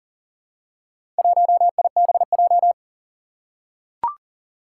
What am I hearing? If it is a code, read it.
Morse code, heard as a keyed tone, about 40 wpm, 700 Hz. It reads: 1IBJ
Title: Four Challenging Letter-Number Combinations 40wpm